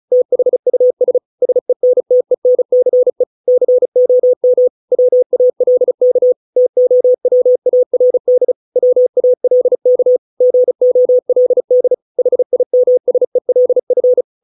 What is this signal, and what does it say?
Morse code, heard as a keyed tone, about 34 wpm, 500 Hz. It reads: THUS SENTENCE COM WALK TOWARD WALK GOLD HIMSELF